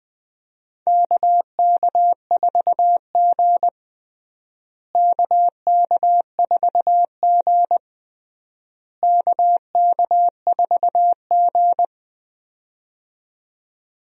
Morse code, heard as a keyed tone, about 20 wpm, 700 Hz. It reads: KK4G KK4G KK4G
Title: USA Call Signs - 4 Characters - Repeated 3x - 20wpm